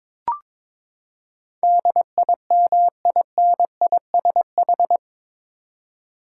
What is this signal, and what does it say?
Morse code, heard as a keyed tone, about 22 wpm, 700 Hz. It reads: DIMINISH
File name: practice / 22wpm / Eight Letter Words - 22wpm